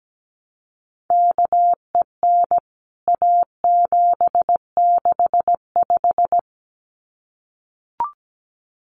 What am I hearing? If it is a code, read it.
Morse code, heard as a keyed tone, about 17 wpm, 700 Hz. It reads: KEN A765
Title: CWT Exchange 17wpm